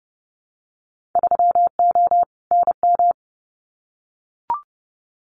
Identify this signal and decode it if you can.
Morse code, heard as a keyed tone, about 30 wpm, 700 Hz. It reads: 3O NM
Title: Winter Field Day Exchanges Encore 30wpm